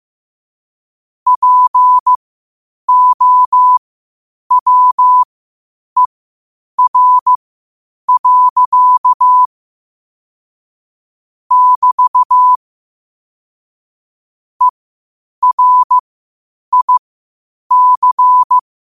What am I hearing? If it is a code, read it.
Morse code, heard as a keyed tone, about 15 wpm, spaced as for 8 wpm, 1000 Hz. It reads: POWER. <BT> ERIC
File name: qotdfiles / 8wpm